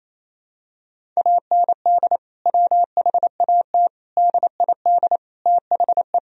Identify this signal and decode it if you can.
Morse code, heard as a keyed tone, about 28 wpm, 700 Hz. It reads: AND WHAT DID THE
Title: Sentences from Top 1000 Words - Review Part 2 - 28wpm